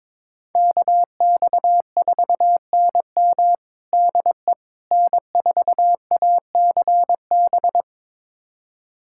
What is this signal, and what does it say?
Morse code, heard as a keyed tone, about 22 wpm, 700 Hz. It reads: KX4NM DE N4ACB
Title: USA Callsign DE Callsign 22wpm